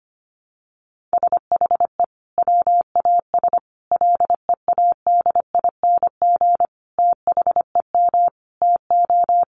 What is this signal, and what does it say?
Morse code, heard as a keyed tone, about 25 wpm, 700 Hz. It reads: SHE WAS LEADING THEM TO
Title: Sentences from Top 1100 Words 25wpm